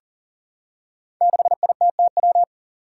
Morse code, heard as a keyed tone, about 40 wpm, 700 Hz. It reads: 6ITTW
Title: Five Challenging Letter-Number Combinations 40wpm